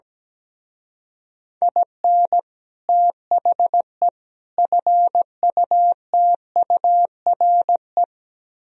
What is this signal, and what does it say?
Morse code, heard as a keyed tone, about 19 wpm, 700 Hz. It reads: IN THE FUTURE
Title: Speed-Racing - Sets of 3 Words 17wpm